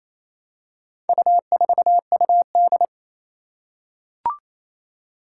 Morse code, heard as a keyed tone, about 28 wpm, 700 Hz. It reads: U4UD
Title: Four Challenging Letter-Number Combinations 28wpm